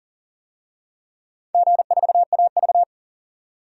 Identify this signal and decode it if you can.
Morse code, heard as a keyed tone, about 40 wpm, 700 Hz. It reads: G4AV
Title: DXCC-100 Call Signs - 4 Characters - 40wpm